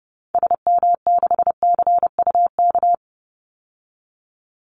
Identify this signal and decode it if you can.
Morse code, heard as a keyed tone, about 30 wpm, 700 Hz. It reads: SM6CUK
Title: Call Signs Repeated 2x - 30wpm